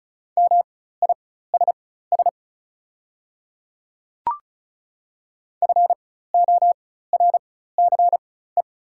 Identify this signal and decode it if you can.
Morse code, heard as a keyed tone, about 33 wpm, spaced as for 16 wpm, 690 Hz. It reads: MISS FORCE
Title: 4x Character Spacing - Top 500 Words - 35wpm